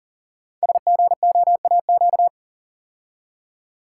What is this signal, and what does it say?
Morse code, heard as a keyed tone, about 38 wpm, 700 Hz. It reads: SGOAQ
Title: Code Groups - 5 Letters - 40wpm